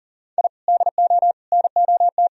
Morse code, heard as a keyed tone, about 40 wpm, 700 Hz. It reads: I DO NOT
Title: Sentences from Top 1000 Words - Review Part 3 - 40wpm